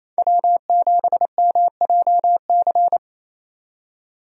Morse code, heard as a keyed tone, about 28 wpm, 700 Hz. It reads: W7MJC
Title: USA Call Signs - 5 Characters - 28wpm